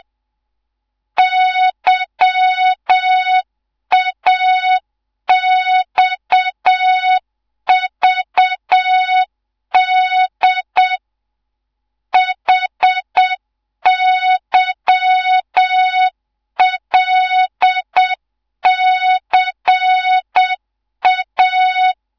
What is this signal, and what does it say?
Morse code, heard as a keyed tone, about 7 wpm, 3700 Hz. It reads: YAXVD HYLCA